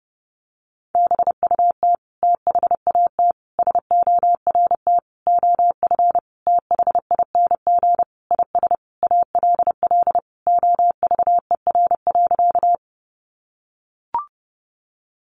Morse code, heard as a keyed tone, about 28 wpm, 700 Hz. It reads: BUT THAT SORT OF THING IS ALL OVER.
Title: Sentences from Top 1300 Words 30wpm